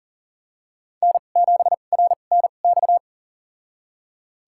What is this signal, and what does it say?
Morse code, heard as a keyed tone, about 40 wpm, spaced as for 29 wpm, 700 Hz. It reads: N7RNX
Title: USA Call Signs - 5 Characters - 2x Character Spacing - 40wpm